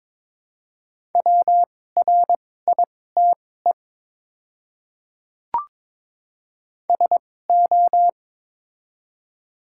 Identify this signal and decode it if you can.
Morse code, heard as a keyed tone, about 22 wpm, spaced as for 16 wpm, 700 Hz. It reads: WRITE SO
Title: 2x Character Spacing - Top 100 Words - 22wpm